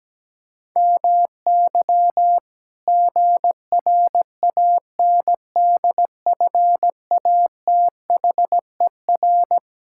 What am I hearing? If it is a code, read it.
Morse code, heard as a keyed tone, about 17 wpm, 700 Hz. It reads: MY GRANDFATHER
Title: Pangram Challenge v2 - 17wpm